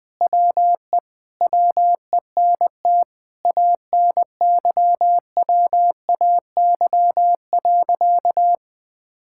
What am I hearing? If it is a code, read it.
Morse code, heard as a keyed tone, about 20 wpm, 700 Hz. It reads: WE WENT ANYWAY.
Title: Sentences with 4-Word Phrases - 20wpm